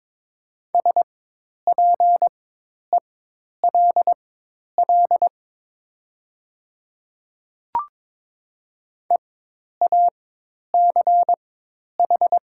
Morse code, heard as a keyed tone, about 22 wpm, spaced as for 10 wpm, 700 Hz. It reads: SPELL EACH
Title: 4x Character Spacing - Top 200 Words - 22wpm